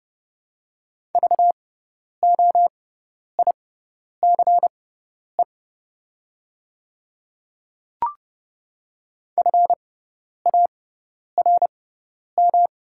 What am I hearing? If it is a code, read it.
Morse code, heard as a keyed tone, about 30 wpm, spaced as for 10 wpm, 700 Hz. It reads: VOICE FARM
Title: 6x Character Spacing - Top 500 Words - 30wpm